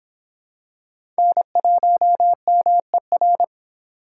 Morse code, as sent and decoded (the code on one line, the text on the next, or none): -. .---- -- . .-.
N1MER